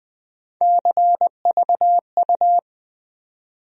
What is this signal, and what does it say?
Morse code, heard as a keyed tone, about 20 wpm, 700 Hz. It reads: CVU
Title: USA Call Sign Postfixes - 3 Letters 20wpm